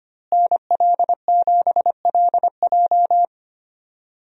Morse code, heard as a keyed tone, about 25 wpm, 700 Hz. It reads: NL7LJ
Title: USA Call Signs - 5 Characters - Repeated 2x - 25wpm